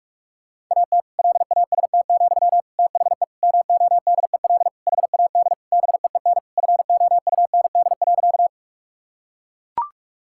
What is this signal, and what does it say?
Morse code, heard as a keyed tone, about 41 wpm, 700 Hz. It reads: AT LAST, THE MODEL HAD BEEN FOUND.
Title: Sentences from Top 1000 Words - Review Part 5 - 45wpm